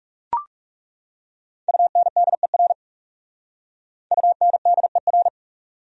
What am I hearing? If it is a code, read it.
Morse code, heard as a keyed tone, about 38 wpm, 700 Hz. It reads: UNDER UNDER
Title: Speed-Racing Top 300 Words 30wpm